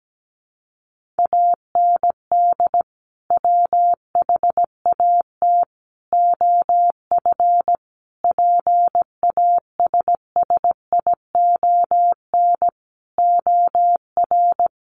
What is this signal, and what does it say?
Morse code, heard as a keyed tone, about 17 wpm, 700 Hz. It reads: AND WHAT OF PASSION OR
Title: Sentences from Top 1400 Words 17wpm